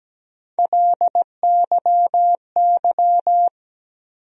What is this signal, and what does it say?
Morse code, heard as a keyed tone, about 17 wpm, 700 Hz. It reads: LYY
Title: USA Call Sign Postfixes - 3 Letters 17wpm